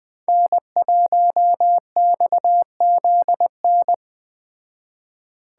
Morse code, heard as a keyed tone, about 20 wpm, 700 Hz. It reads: N1XZN
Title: Speed Racing - USA 1x3 Call Signs 17wpm